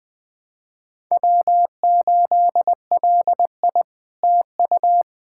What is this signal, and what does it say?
Morse code, heard as a keyed tone, about 20 wpm, 700 Hz. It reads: W8LI TU
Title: SOTA QSO Practice 20wpm